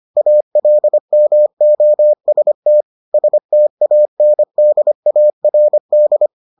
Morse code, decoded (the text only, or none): ALMOST STANDARD